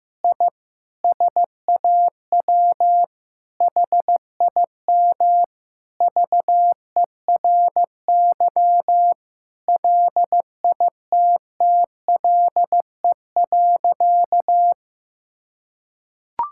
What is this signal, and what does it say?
Morse code, heard as a keyed tone, about 16 wpm, 700 Hz. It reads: I SAW HIM VERY LITTLE.
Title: Sentences from Top 1000 Words - Review Part 5 - 15wpm